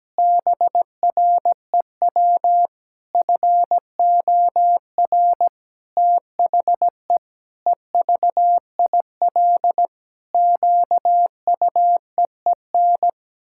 Morse code, heard as a keyed tone, about 17 wpm, 700 Hz. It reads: BREW FOR THE EVIL QUEEN
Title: Pangram Challenge v2 - 17wpm